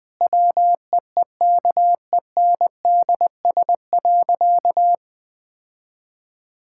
Morse code, heard as a keyed tone, about 20 wpm, 700 Hz. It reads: WEEKENDS.
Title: Difficult Sentences with Hints - 20wpm